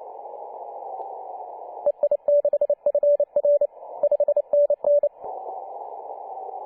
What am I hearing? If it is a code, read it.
Morse code, heard as a keyed tone, about 29 wpm, 580 Hz. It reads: EI6FR 5NN
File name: 2013/Sep/05 2136z T8GM 14.023MHz CW